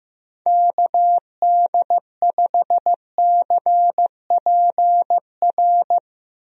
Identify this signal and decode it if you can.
Morse code, heard as a keyed tone, about 15 wpm, 700 Hz. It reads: KD5CPR
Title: USA Call Signs - 6 Characters - Repeated 2x - 15wpm